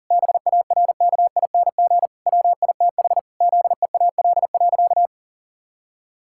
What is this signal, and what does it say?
Morse code, heard as a keyed tone, about 40 wpm, 700 Hz. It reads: BARKING WITH ZEAL.
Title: Pangram Challenge v2 - 40wpm